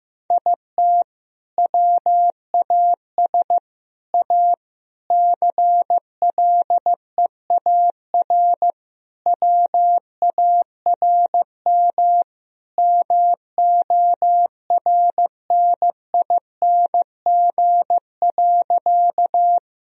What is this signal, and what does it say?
Morse code, heard as a keyed tone, about 15 wpm, 700 Hz. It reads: IT WAS A CLEAR WARM MORNING.
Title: Sentences from Top 1000 Words - Review Part 2 - 15wpm